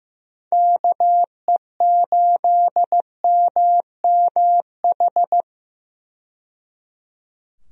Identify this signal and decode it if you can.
Morse code, heard as a keyed tone, about 15 wpm, 700 Hz. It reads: KE8MMH
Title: USA Call Signs - 6 Characters - 15wpm